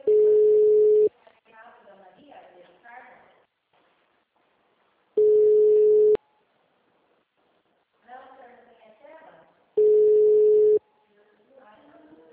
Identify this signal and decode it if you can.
Ringback tone (Europe, 425 Hz), with speech.